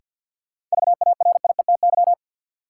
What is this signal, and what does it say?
Morse code, heard as a keyed tone, about 51 wpm, 700 Hz. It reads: VARIETY
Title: Sentences from Top 1000 Words - Review Part 4 - New Word Drill - 50wpm